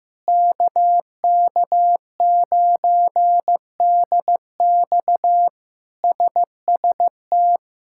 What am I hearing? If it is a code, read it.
Morse code, heard as a keyed tone, about 15 wpm, 700 Hz. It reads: KK9DX SST
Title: K1USN SST QSO Practice - 15wpm